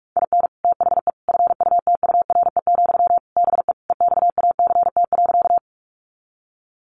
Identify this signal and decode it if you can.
Morse code, heard as a keyed tone, about 44 wpm, 700 Hz. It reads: IN THE FUTURE, BE EXACT.
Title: Sentences from Top 1300 Words 45wpm